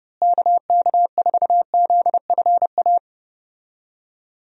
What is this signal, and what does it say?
Morse code, heard as a keyed tone, about 30 wpm, 700 Hz. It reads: KK4ZFA
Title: USA Call Signs - 6 Characters - Repeated 3x - 30wpm